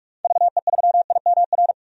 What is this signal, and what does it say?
Morse code, heard as a keyed tone, about 43 wpm, 700 Hz. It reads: VE3IMR
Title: DXCC-100 Call Signs - 6 Characters - 45wpm